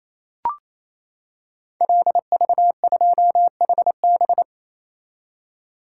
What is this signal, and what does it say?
Morse code, heard as a keyed tone, about 27 wpm, 700 Hz. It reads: LV2HB
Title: Code Groups - 5 Letter:Numbers - 28wpm